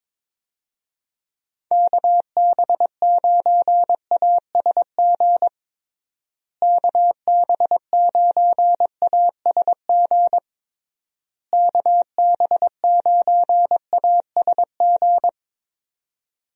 Morse code, heard as a keyed tone, about 22 wpm, 700 Hz. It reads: KB9ASG KB9ASG KB9ASG
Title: USA Call Signs - 6 Characters - Repeated 3x - 22wpm